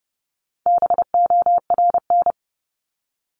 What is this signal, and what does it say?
Morse code, heard as a keyed tone, about 30 wpm, 700 Hz. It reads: BORN
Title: Top 751-1000 Words 30wpm